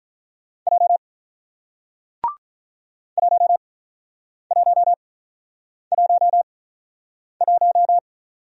Morse code, heard as a keyed tone, about 47 wpm, spaced as for 9 wpm, 700 Hz. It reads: J 1111